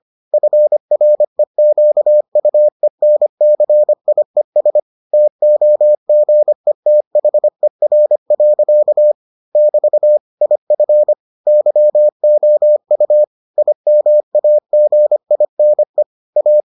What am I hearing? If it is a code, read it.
Morse code, heard as a keyed tone, about 25 wpm, 600 Hz. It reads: FREQUENCIES TOGETHER. <BT> IF YOU IMAGINE A